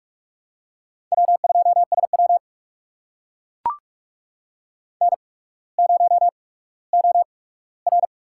Morse code, heard as a keyed tone, about 42 wpm, spaced as for 19 wpm, 700 Hz. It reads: W2SW N0OR
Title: USA Call Signs - 4 Characters - 8x Character Spacing - 45wpm